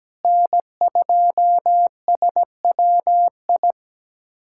Morse code, heard as a keyed tone, about 17 wpm, 700 Hz. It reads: N2SWI
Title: USA Callsign DE Callsign - 5 Characters - 17wpm